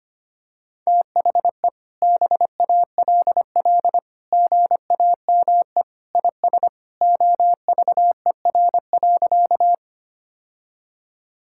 Morse code, heard as a keyed tone, about 25 wpm, 700 Hz. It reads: THE BALL GAME IS OVER.
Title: Sentences from Top 1000 Words - Review Part 6 - 25wpm